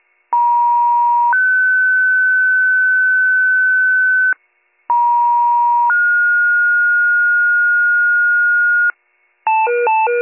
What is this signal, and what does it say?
Two-tone page, heard as a one-second tone, then a three-second tone, about 950 then 1500 Hz, about 950 then 1430 Hz, with noise. It is heard from 0.3 to 8.9 s.